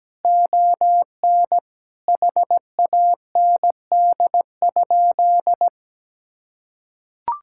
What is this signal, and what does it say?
Morse code, heard as a keyed tone, about 17 wpm, 700 Hz. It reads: ON HAND?
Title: Sentences from Top 1000 Words - Review Part 5 - 17wpm